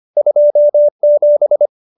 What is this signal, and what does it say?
Morse code, heard as a keyed tone, about 25 wpm, 590 Hz. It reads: 27
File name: qso207b 25